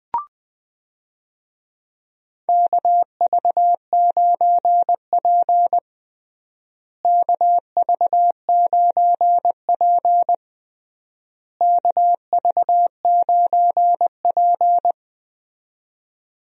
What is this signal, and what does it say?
Morse code, heard as a keyed tone, about 20 wpm, 700 Hz. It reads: KV9P KV9P KV9P
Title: USA Call Signs - 4 Characters - Repeated 3x - 20wpm